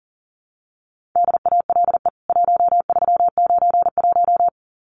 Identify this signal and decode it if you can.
Morse code, heard as a keyed tone, about 40 wpm, 700 Hz. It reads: DALE 1391
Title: CWT Exchange 40wpm